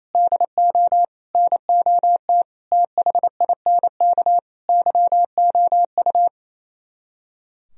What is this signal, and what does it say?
Morse code, heard as a keyed tone, about 34 wpm, 700 Hz. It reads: DO NOT THINK YOU
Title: Speed-Racing - Sets of 5 Words 28wpm